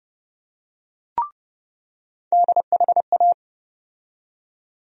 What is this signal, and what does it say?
Morse code, heard as a keyed tone, about 30 wpm, 700 Hz. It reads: DHA
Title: USA Call Sign Postfixes - 3 Letters 30wpm